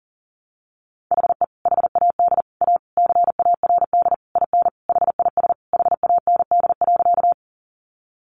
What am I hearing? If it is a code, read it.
Morse code, heard as a keyed tone, about 40 wpm, 700 Hz. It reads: HE HAD A CARD IN HIS HAND.
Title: Sentences from Top 600 Words 40wpm